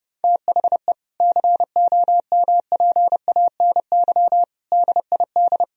Morse code, heard as a keyed tone, about 30 wpm, 700 Hz. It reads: THE COMPANY DID
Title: Sentences from Top 1000 Words - Review Part 3 - 30wpm